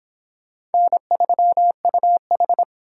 Morse code, heard as a keyed tone, about 23 wpm, 700 Hz. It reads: N3UH